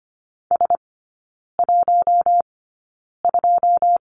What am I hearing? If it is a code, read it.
Morse code, heard as a keyed tone, about 25 wpm, spaced as for 9 wpm, 700 Hz. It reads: S12
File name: Lesson 14e - Number 2 - Three Characters 25wpm